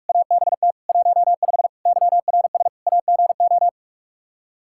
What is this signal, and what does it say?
Morse code, heard as a keyed tone, about 46 wpm, 700 Hz. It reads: ABT 15 YRS AGO
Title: Practice QSO - Exchange 3 - More Info - 45wpm